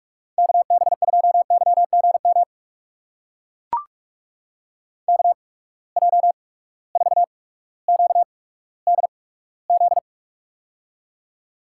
Morse code, heard as a keyed tone, about 43 wpm, spaced as for 19 wpm, 700 Hz. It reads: KB2YGM KJ4QDZ
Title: USA Call Signs - 6 Characters - 8x Character Spacing - 45wpm